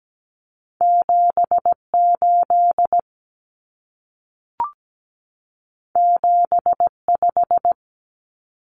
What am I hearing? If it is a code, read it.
Morse code, heard as a keyed tone, about 17 wpm, 700 Hz. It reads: 78 75
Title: Common Cut Numbers - 2-Digit 17wpm